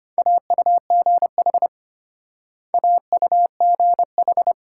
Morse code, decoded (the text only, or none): AUGH AUGH